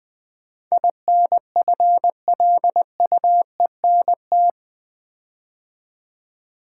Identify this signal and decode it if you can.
Morse code, heard as a keyed tone, about 20 wpm, 700 Hz. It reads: INFLUENT
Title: Build A Word - 4 Letters At A Time - 11-Letter Words - 20wpm